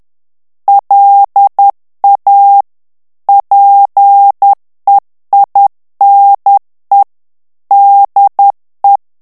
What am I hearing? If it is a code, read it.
Morse code, heard as a keyed tone, about 10 wpm, 800 Hz. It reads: LA PEINE DE
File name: morse mp3 / FRANCAIS TEXTE CLAIR 2 VITESSE 10 MOTS MINUTES